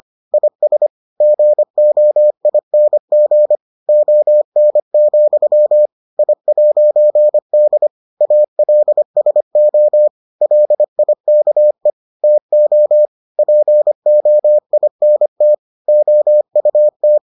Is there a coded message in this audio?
Yes: Morse code from 0.3 to 17.2 s, heard as a keyed tone, about 25 wpm, 600 Hz. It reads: IS GOING ON, I'D ALSO LIKE TO POINT OUT